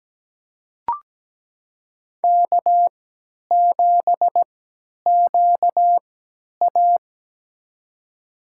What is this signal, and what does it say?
Morse code, heard as a keyed tone, about 17 wpm, spaced as for 10 wpm, 700 Hz. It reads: K7QA
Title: USA Call Signs - 3x Character Spacing - 17wpm